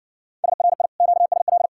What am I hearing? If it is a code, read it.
Morse code, heard as a keyed tone, about 60 wpm, 700 Hz. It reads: SRI QSL